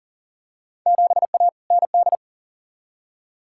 Morse code, heard as a keyed tone, about 40 wpm, 700 Hz. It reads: 7A ND
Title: Field Day Exchanges Encore - 40wpm